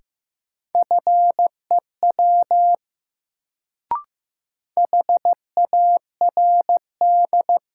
Morse code, heard as a keyed tone, about 15 wpm, 700 Hz. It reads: FEW HARD
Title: Sentences from Top 300 Words - New Word Drill - 15wpm